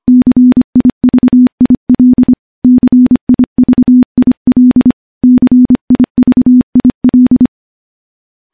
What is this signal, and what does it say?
Morse code, heard as a keyed tone, about 25 wpm, 250 Hz. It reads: CIVIL CIVIL CIVIL
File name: morse / morse-training-25WPM